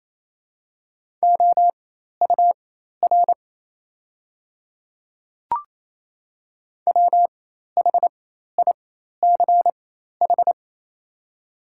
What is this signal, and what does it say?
Morse code, heard as a keyed tone, about 28 wpm, spaced as for 13 wpm, 700 Hz. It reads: OUR WHICH
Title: 4x Character Spacing - Top 300 Words - 28wpm